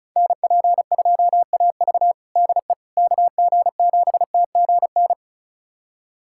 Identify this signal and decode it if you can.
Morse code, heard as a keyed tone, about 36 wpm, 700 Hz. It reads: NP2AV DE KG7TGN
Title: USA Callsign DE Callsign 35wpm